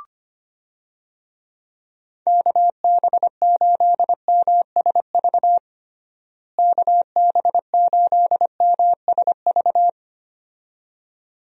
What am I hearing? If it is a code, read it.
Morse code, heard as a keyed tone, about 25 wpm, 700 Hz. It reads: KB8MSV KB8MSV
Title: USA Call Signs - 6 Characters - Repeated 2x - 25wpm